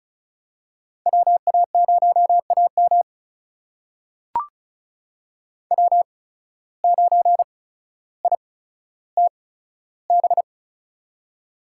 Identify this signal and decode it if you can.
Morse code, heard as a keyed tone, about 36 wpm, spaced as for 15 wpm, 700 Hz. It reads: WA0AM W9ITB
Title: USA Call Signs - 5 Characters - 8x Character Spacing - 35wpm